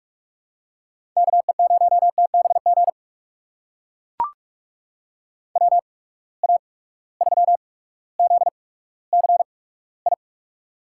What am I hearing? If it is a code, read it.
Morse code, heard as a keyed tone, about 42 wpm, spaced as for 19 wpm, 700 Hz. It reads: KE0TBG WA3ZCI
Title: USA Call Signs - 6 Characters - 8x Character Spacing - 45wpm